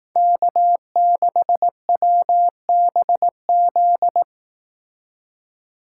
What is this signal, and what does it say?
Morse code, heard as a keyed tone, about 18 wpm, 700 Hz. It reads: K6WBZ
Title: Speed Racing - USA Call Signs 15wpm